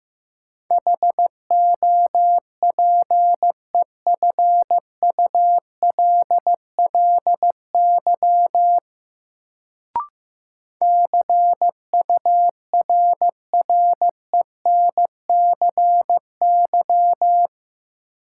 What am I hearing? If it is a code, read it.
Morse code, heard as a keyed tone, about 15 wpm, 700 Hz. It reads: HOPEFULLY CURRENCY
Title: Sentences from Top 1500 Words - New Word Drill Alt - 15wpm